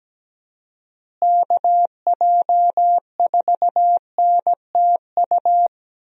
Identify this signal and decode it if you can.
Morse code, heard as a keyed tone, about 17 wpm, 700 Hz. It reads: KJ4NTU
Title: USA Call Signs - 6 Characters - Repeated 3x - 17wpm